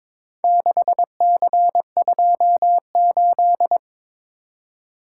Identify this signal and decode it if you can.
Morse code, heard as a keyed tone, about 22 wpm, 700 Hz. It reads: 6C28
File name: practice / 22wpm / Four Letter-Number Combinations - 22wpm